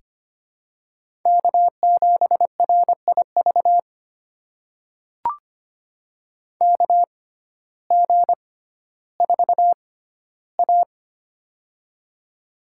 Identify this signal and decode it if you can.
Morse code, heard as a keyed tone, about 25 wpm, spaced as for 14 wpm, 700 Hz. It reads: K7RIV KG4A
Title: USA Call Signs - 6x Character Spacing - 25wpm